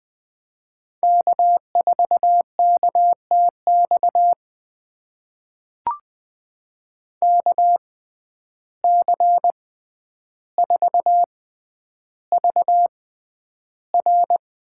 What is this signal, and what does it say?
Morse code, heard as a keyed tone, about 20 wpm, spaced as for 10 wpm, 700 Hz. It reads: K4KTX KC4VR
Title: USA Call Signs - 5 Characters - 6x Character Spacing - 20wpm